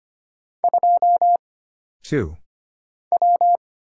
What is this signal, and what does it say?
Morse code, heard as a keyed tone, about 25 wpm, 700 Hz. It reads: 2 W
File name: Single Character (Letter:Number:Punctuation:Prosign) - Mind-Melt - 25wpm